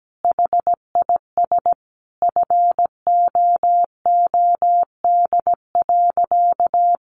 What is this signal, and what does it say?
Morse code, heard as a keyed tone, about 17 wpm, 700 Hz. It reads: HIS FOOD.
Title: Sentences from Top 1100 Words 17wpm